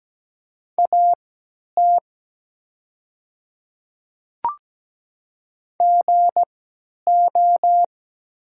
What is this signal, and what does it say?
Morse code, heard as a keyed tone, about 17 wpm, spaced as for 10 wpm, 700 Hz. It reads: AT GO
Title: 3x Character Spacing - Top 100 Words - 17wpm